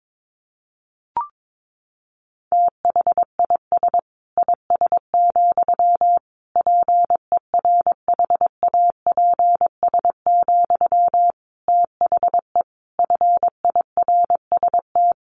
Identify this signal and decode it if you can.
Morse code, heard as a keyed tone, about 22 wpm, 700 Hz. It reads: THIS IS, PERHAPS, THE FIRST
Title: Sentences from Top 1500 Words 22wpm